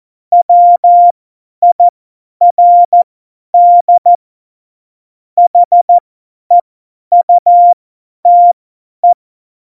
Morse code, heard as a keyed tone, about 14 wpm, spaced as for 10 wpm, 700 Hz. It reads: WIRD HEUTE
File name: Morsekurs / Witze / Witz 27 Juni 17, Tempo 50 F70